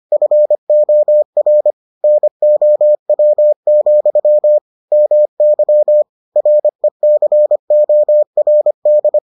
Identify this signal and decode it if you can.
Morse code, heard as a keyed tone, about 25 wpm, 600 Hz. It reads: FOR NOW, MY RECORD